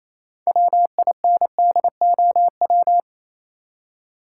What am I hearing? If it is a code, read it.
Morse code, heard as a keyed tone, about 27 wpm, 700 Hz. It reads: WINDOW
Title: Six Letter Words - 28wpm